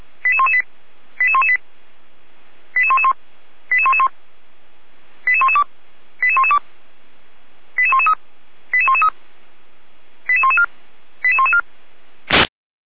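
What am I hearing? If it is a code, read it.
Five-tone selective call (ZVEI), heard as quick runs of five tones, tones of about 70 ms. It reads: 88118 88118 88111 88111 88112 88112 88113 88113 88114 88114
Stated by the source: ZVEI five-tone call received by SDR, from the Feuerwehr Orlamünde (fire brigade) server audio.f85360.feuerwehr-orlamuende.org.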